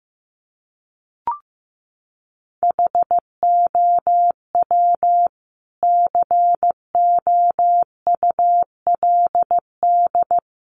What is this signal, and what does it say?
Morse code, heard as a keyed tone, about 15 wpm, 700 Hz. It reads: HOW COULD
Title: Sentences from Top 300 Words 15wpm